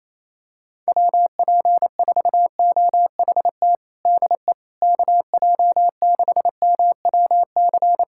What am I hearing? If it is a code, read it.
Morse code, heard as a keyed tone, about 28 wpm, 700 Hz. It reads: WP4OHT DE KJ6MWC
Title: USA Callsign DE Callsign 28wpm